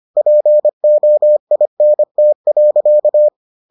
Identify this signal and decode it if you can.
Morse code, heard as a keyed tone, about 25 wpm, 600 Hz. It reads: POINT.